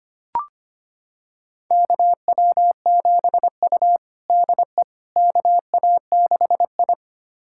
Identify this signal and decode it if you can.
Morse code, heard as a keyed tone, about 25 wpm, 700 Hz. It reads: KW7U DE KA6I
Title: USA Callsign DE Callsign - 4 Characters - 25wpm